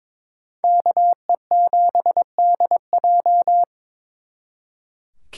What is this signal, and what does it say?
Morse code, heard as a keyed tone, about 27 wpm, 700 Hz. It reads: KE7DJ
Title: Speed Racing - USA 2x2 Call Signs 22wpm